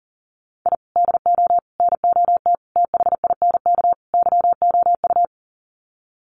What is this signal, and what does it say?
Morse code, heard as a keyed tone, about 40 wpm, 700 Hz. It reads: I DO NOT THINK YOU
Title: Sets of 5 Words 40wpm